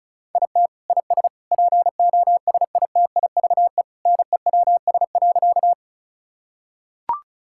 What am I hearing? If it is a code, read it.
Morse code, heard as a keyed tone, about 34 wpm, 700 Hz. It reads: IT IS POSITIVE NEWS.
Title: Sentences from Top 1000 Words - Review Part 5 - 35wpm